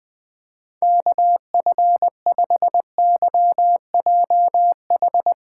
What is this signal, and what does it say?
Morse code, heard as a keyed tone, about 20 wpm, 700 Hz. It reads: KF5YJH
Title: USA Callsign DE Callsign - 6 Characters - 20wpm